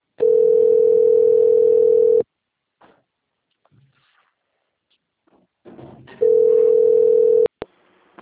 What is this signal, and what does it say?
Ringback tone (North America), 440 + 480 Hz.
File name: rg-1600-unknown- - - .19054